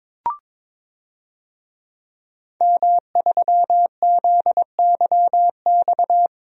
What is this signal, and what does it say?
Morse code, heard as a keyed tone, about 22 wpm, 700 Hz. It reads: M3ZYX
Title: DXCC-100 Call Signs - 5 Characters - 22wpm